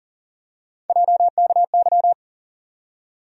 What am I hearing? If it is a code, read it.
Morse code, heard as a keyed tone, about 40 wpm, 700 Hz. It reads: JKY
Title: USA Call Sign Postfixes - 3 Letters 40wpm